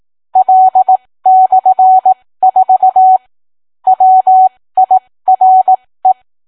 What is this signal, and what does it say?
Morse code, heard as a keyed tone, about 18 wpm, spaced as for 15 wpm, 750 Hz. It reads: L/4 WIRE